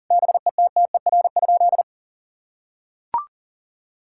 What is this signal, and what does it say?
Morse code, heard as a keyed tone, about 40 wpm, 700 Hz. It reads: BETTER?
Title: Sentences from Top 1000 Words - Review Part 6 - 40wpm